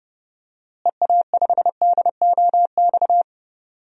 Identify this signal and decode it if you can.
Morse code, heard as a keyed tone, about 30 wpm, 700 Hz. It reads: EA5DOX
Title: DXCC-100 Call Signs - 6 Characters - Repeated 2x - 30wpm